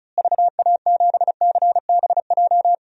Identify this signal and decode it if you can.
Morse code, heard as a keyed tone, about 36 wpm, 700 Hz. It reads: VA7CBJ
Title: DXCC-100 Call Signs - 6 Characters - Repeated 3x - 35wpm